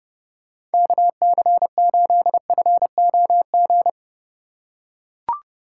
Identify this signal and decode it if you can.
Morse code, heard as a keyed tone, about 30 wpm, 700 Hz. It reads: KC8FOG
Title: USA Call Signs - 2x Character Spacing - 30wpm